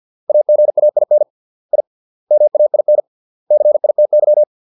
Morse code, heard as a keyed tone, about 50 wpm, 600 Hz. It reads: AGAIN I MAIN CITY